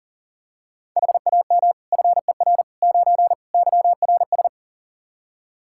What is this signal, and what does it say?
Morse code, heard as a keyed tone, about 40 wpm, 700 Hz. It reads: HAM FER 9 YRS
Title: K1USN SST QSO Practice 40wpm